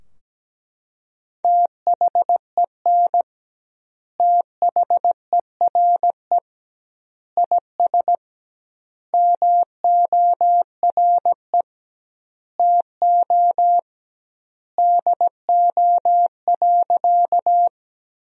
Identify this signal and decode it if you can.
Morse code, heard as a keyed tone, about 17 wpm, 700 Hz. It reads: THEN THERE IS MORE TO DO.
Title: 2x Word-Spacing - Sentences from Top 100 Words - 17wpm